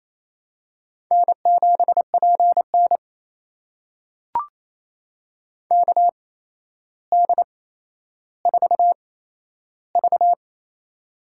Morse code, heard as a keyed tone, about 28 wpm, spaced as for 12 wpm, 700 Hz. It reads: N7PN KD4V
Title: USA Call Signs - 4 Characters - 8x Character Spacing - 28wpm